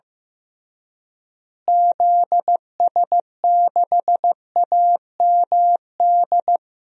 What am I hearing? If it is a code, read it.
Morse code, heard as a keyed tone, about 15 wpm, 700 Hz. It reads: ZS6AMD
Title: DXCC-100 Call Signs - 6 Characters - Repeated 2x - 15wpm